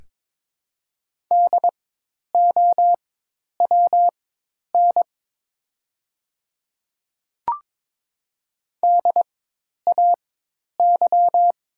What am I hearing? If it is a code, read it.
Morse code, heard as a keyed tone, about 22 wpm, spaced as for 10 wpm, 700 Hz. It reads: DOWN DAY
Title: 4x Character Spacing - Top 100 Words - 22wpm